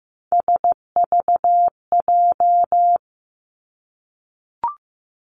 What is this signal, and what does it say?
Morse code, heard as a keyed tone, about 15 wpm, 700 Hz. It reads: SVJ
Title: Three Challenging Letter-Number Combinations 15wpm